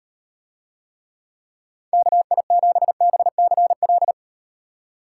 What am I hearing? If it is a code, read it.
Morse code, heard as a keyed tone, about 38 wpm, 700 Hz. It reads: KI7BCL